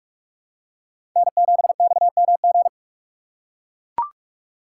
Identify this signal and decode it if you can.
Morse code, heard as a keyed tone, about 44 wpm, 700 Hz. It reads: N7XMG